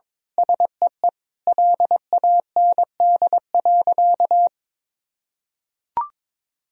Morse code, heard as a keyed tone, about 21 wpm, 700 Hz. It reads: SEE LAND.
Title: Sentences from Top 1000 Words - Review Part 2 - 22wpm